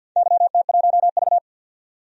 Morse code, heard as a keyed tone, about 53 wpm, 700 Hz. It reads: YT1V
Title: DXCC-100 Call Signs - 4 Characters - 50wpm